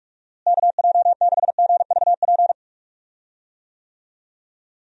Morse code, heard as a keyed tone, about 46 wpm, 700 Hz. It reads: KJ6GVP
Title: USA Call Signs - 6 Characters - Repeated 3x - 45wpm